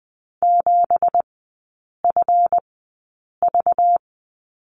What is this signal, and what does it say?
Morse code, heard as a keyed tone, about 20 wpm, spaced as for 8 wpm, 700 Hz. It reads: 7FV
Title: 198 Lesson 32e - Letter k - Three Characters 20wpm